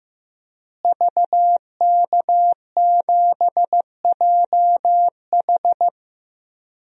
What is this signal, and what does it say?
Morse code, heard as a keyed tone, about 15 wpm, 700 Hz. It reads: VK7JH
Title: DXCC-100 Call Signs - 5 Characters - Repeated 2x - 15wpm